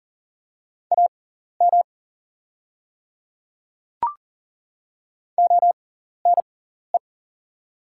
Morse code, heard as a keyed tone, about 37 wpm, spaced as for 14 wpm, 700 Hz. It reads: AM ONE
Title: 6x Character Spacing - Top 500 Words - 40wpm